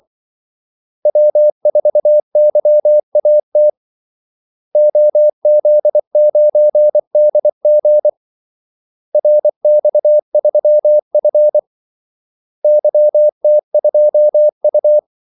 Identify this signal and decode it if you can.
Morse code, heard as a keyed tone, about 24 wpm, 600 Hz. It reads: W4YAT OZ9DG RX3F YT2U
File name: Callsigns-24wpm